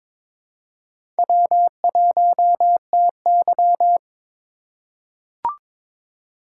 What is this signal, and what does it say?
Morse code, heard as a keyed tone, about 22 wpm, 700 Hz. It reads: W1TY
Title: USA Call Signs - 4 Characters - 2x Character Spacing - 22wpm